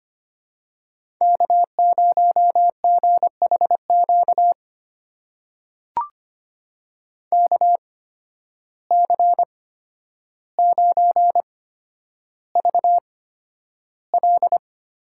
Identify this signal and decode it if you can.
Morse code, heard as a keyed tone, about 25 wpm, spaced as for 11 wpm, 700 Hz. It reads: K0GHQ KC9VL
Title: USA Call Signs - 5 Characters - 8x Character Spacing - 25wpm